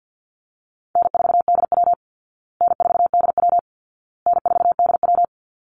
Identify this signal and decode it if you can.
Morse code, heard as a keyed tone, about 50 wpm, 700 Hz. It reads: N4DW N4DW N4DW
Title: Call Signs Repeated 2x - 50wpm